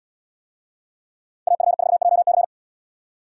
Morse code, heard as a keyed tone, about 72 wpm, 700 Hz. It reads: AF4JC